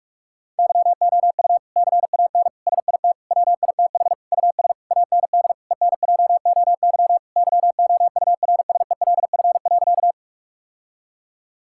Morse code, heard as a keyed tone, about 46 wpm, 700 Hz. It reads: YOU CAN SIT WITH US AND ENJOY YOURSELF.
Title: Sentences from Top 1000 Words - Review Part 5 - 45wpm